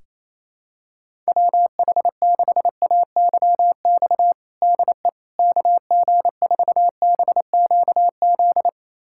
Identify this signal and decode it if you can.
Morse code, heard as a keyed tone, about 28 wpm, 700 Hz. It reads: WH6AYX DE KG4BQZ
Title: USA Callsign DE Callsign 28wpm